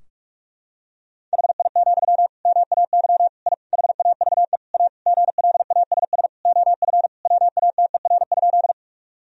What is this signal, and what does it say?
Morse code, heard as a keyed tone, about 48 wpm, 700 Hz. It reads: HI, MAY I HAVE A GLASS OF WATER?
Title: Sentences from Top 1000 Words - Review Part 3 - 45wpm